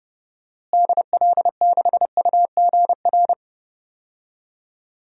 Morse code, heard as a keyed tone, about 32 wpm, 700 Hz. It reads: DL6UGR
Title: DXCC-100 Call Signs - 6 Characters - Repeated 3x - 30wpm